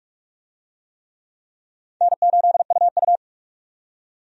Morse code, heard as a keyed tone, about 46 wpm, 700 Hz. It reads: N8UU
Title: USA Call Signs - 4 Characters - 45wpm